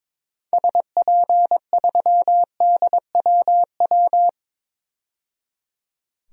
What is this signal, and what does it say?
Morse code, heard as a keyed tone, about 22 wpm, 700 Hz. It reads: SP3DWW
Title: DXCC-100 Call Signs - 6 Characters - 22wpm